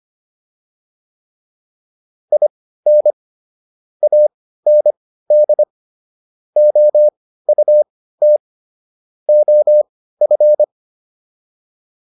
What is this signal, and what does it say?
Morse code, heard as a keyed tone, about 25 wpm, spaced as for 15 wpm, 600 Hz. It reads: IN AND OUT OF